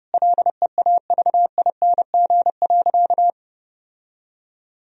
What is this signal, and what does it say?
Morse code, heard as a keyed tone, about 30 wpm, 700 Hz. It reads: LEAVING.
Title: Sentences from Top 1000 Words - Review Part 4 - 30wpm